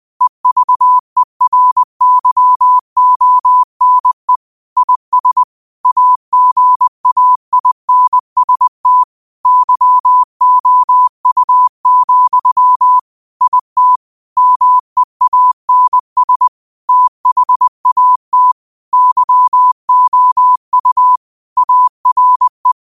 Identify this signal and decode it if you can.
Morse code, heard as a keyed tone, about 20 wpm, 1000 Hz. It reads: EVERYONE IS AGAINST YOU, IT MEANS THAT YOU ARE